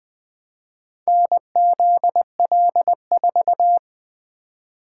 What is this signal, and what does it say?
Morse code, heard as a keyed tone, about 21 wpm, 700 Hz. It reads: NZL4
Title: Four Letter-Number Combinations - 20wpm